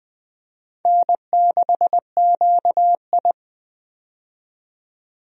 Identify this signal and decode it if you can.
Morse code, heard as a keyed tone, about 21 wpm, 700 Hz. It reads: N6QI
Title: USA Call Signs - 4 Characters - 20wpm